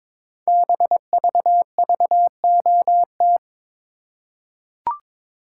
Morse code, heard as a keyed tone, about 22 wpm, 700 Hz. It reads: BVVOT
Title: Code Groups - 5 Letter:Numbers - 22wpm